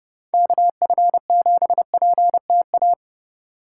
Morse code, heard as a keyed tone, about 30 wpm, 700 Hz. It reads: KF7PTA